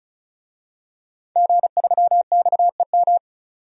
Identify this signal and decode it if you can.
Morse code, heard as a keyed tone, about 36 wpm, 700 Hz. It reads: G3XEM
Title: DXCC-100 Call Signs - 5 Characters - 35wpm